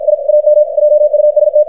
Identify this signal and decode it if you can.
Morse code, heard as a keyed tone, about 53 wpm, 600 Hz, with radio noise. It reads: I AM JUST
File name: archived / CRSnet